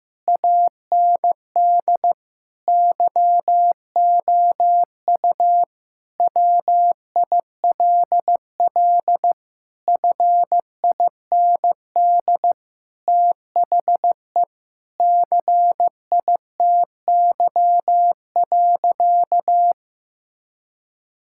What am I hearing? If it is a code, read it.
Morse code, heard as a keyed tone, about 16 wpm, 700 Hz. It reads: AND YOU WILL FIND THE CITY.
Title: Sentences from Top 1000 Words - Review Part 6 - 15wpm